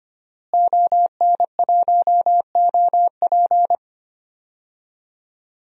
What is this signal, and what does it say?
Morse code, heard as a keyed tone, about 25 wpm, 700 Hz. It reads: ON1OP
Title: DXCC-100 Call Signs - 5 Characters - Repeated 2x - 25wpm